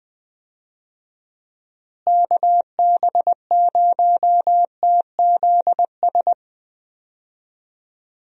Morse code, heard as a keyed tone, about 20 wpm, 700 Hz. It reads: KB0TZS
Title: USA Call Signs - 6 Characters - Repeated 3x - 20wpm